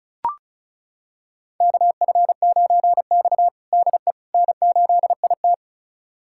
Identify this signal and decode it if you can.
Morse code, heard as a keyed tone, about 34 wpm, 700 Hz. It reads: KF9X DE N8IT
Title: USA Callsign DE Callsign - 4 Characters - 35wpm